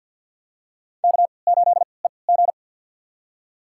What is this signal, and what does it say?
Morse code, heard as a keyed tone, about 48 wpm, spaced as for 28 wpm, 700 Hz. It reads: K8EG
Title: USA Call Signs - 4 Characters - 3x Character Spacing - 50wpm